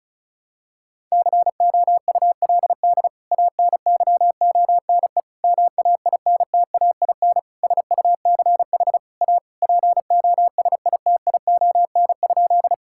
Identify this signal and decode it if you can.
Morse code, heard as a keyed tone, about 36 wpm, 700 Hz. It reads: COULD ANYONE MAINTAIN SUCH A POSITION?
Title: Sentences from Top 1000 Words - Review Part 5 - 35wpm